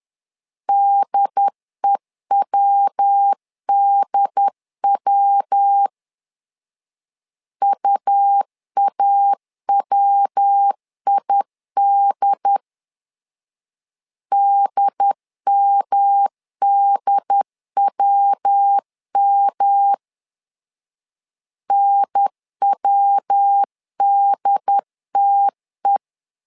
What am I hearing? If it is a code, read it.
Morse code, heard as a keyed tone, about 10 wpm, 800 Hz. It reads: DEWDW UAWID DMDWM NWDTE
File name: LEZIONI / 3 MWRO / 3 MW 10wpm 1959